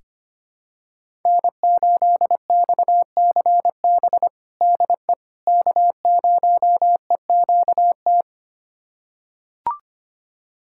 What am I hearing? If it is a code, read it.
Morse code, heard as a keyed tone, about 25 wpm, 700 Hz. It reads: N8XCB DE K0EQT
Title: USA Callsign DE Callsign - 5 Characters - 25wpm